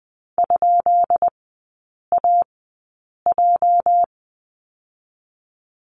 Morse code, heard as a keyed tone, about 20 wpm, spaced as for 8 wpm, 700 Hz. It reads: ?AJ
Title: 205 Lesson 33e - Letter j - Three Characters 20wpm